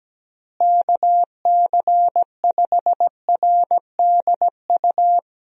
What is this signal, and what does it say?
Morse code, heard as a keyed tone, about 17 wpm, 700 Hz. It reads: KC5RDU
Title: USA Call Signs - 6 Characters - Repeated 3x - 17wpm